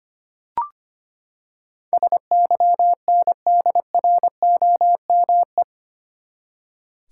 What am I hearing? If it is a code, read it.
Morse code, heard as a keyed tone, about 25 wpm, 700 Hz. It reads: SYNDROME
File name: Eight Letter Words - 25wpm